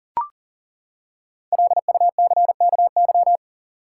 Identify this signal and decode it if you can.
Morse code, heard as a keyed tone, about 40 wpm, 700 Hz. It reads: LUCKY
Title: Five Letter Words - 40wpm